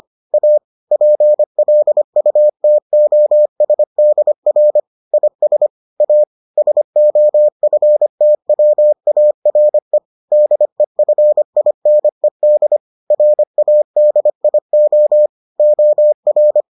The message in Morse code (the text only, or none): A PLUTOSDR IS A SOFTWARE DEFINED RADIO OR